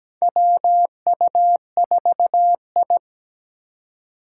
Morse code, heard as a keyed tone, about 17 wpm, 700 Hz. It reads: WU4I
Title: USA Call Signs - 4 Characters - 3x Character Spacing - 17wpm